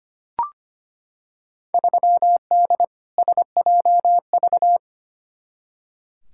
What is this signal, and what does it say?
Morse code, heard as a keyed tone, about 25 wpm, 700 Hz. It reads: 3D SJV
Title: Field Day Exchanges - 25wpm